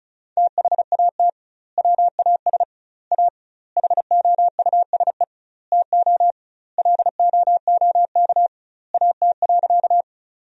Morse code, heard as a keyed tone, about 36 wpm, 700 Hz. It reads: THAT WAS A HOUSE TO LOOK AT.
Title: 2x Word-Spacing - Sentences from Top 200 Words - 35wpm